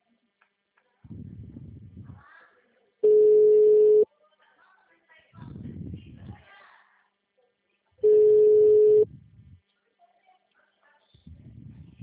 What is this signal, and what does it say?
Ringback tone (Europe, 425 Hz), with speech.